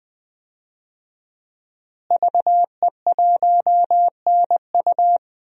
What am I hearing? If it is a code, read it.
Morse code, heard as a keyed tone, about 20 wpm, 700 Hz. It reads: VE1NU
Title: DXCC-100 Call Signs - 5 Characters - Repeated 2x - 20wpm